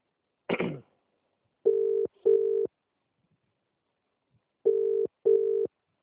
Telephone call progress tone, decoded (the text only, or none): ringback tone